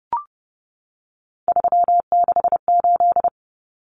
Morse code, heard as a keyed tone, about 30 wpm, 700 Hz. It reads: 368
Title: Numbers - 3-Digit 30wpm